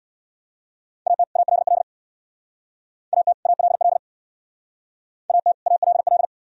Morse code, heard as a keyed tone, about 62 wpm, 700 Hz. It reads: AT ALL AT ALL AT ALL
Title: Speed-Racing - Sets of 2 Words 50wpm